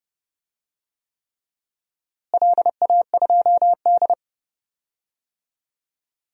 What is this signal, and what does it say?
Morse code, heard as a keyed tone, about 28 wpm, 700 Hz. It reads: LA2D
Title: DXCC-100 Call Signs - 4 Characters - Repeated 3x - 30wpm